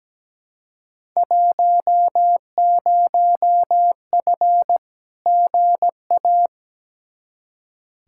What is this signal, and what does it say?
Morse code, heard as a keyed tone, about 17 wpm, 700 Hz. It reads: 10F GA
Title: Field Day Exchanges Encore - 17wpm